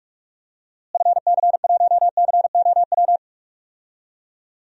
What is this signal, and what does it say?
Morse code, heard as a keyed tone, about 42 wpm, 700 Hz. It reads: FC1COW
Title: DXCC-100 Call Signs - 6 Characters - Repeated 3x - 45wpm